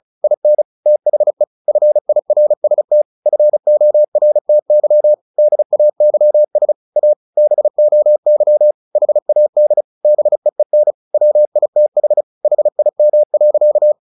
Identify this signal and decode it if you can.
Morse code, heard as a keyed tone, about 34 wpm, 600 Hz. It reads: IN THE FIRST FORTY DAYS A BOY HAD BEEN WITH HIM.
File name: the old man and the sea / the old man and the sea 1 35wpm